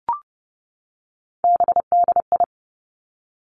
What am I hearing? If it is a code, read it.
Morse code, heard as a keyed tone, about 30 wpm, 700 Hz. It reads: BDI
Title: Three Challenging Letter-Number Combinations 30wpm